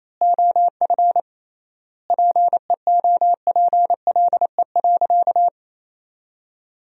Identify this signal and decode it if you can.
Morse code, heard as a keyed tone, about 28 wpm, 700 Hz. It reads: OF PEOPLE.
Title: 3x Word-Spacing - Sentences from Top 500 Words - 28wpm